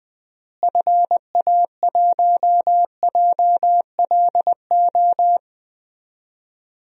Morse code, heard as a keyed tone, about 20 wpm, 700 Hz. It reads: FA1JLO
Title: DXCC-100 Call Signs - 6 Characters - Repeated 2x - 20wpm